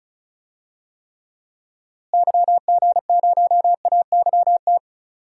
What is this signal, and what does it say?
Morse code, heard as a keyed tone, about 34 wpm, 700 Hz. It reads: YG0AYT
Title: DXCC-100 Call Signs - 6 Characters - Repeated 3x - 35wpm